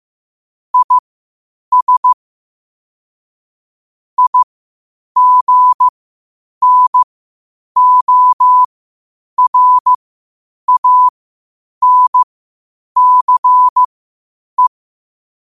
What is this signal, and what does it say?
Morse code, heard as a keyed tone, about 15 wpm, spaced as for 8 wpm, 1000 Hz. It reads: IS IGNORANCE
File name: qotdfiles / 8wpm